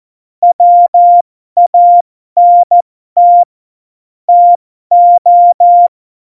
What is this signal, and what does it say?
Morse code, heard as a keyed tone, about 14 wpm, spaced as for 12 wpm, 700 Hz. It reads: WANT TO
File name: Morsekurs / Witze